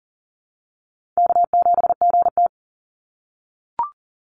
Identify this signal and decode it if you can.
Morse code, heard as a keyed tone, about 40 wpm, 700 Hz. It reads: K7GT
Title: NA SOTA Chaser Callsigns 40wpm